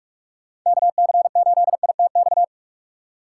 Morse code, heard as a keyed tone, about 46 wpm, 700 Hz. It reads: KC8ITX
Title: USA Call Signs - 6x Character Spacing - 45wpm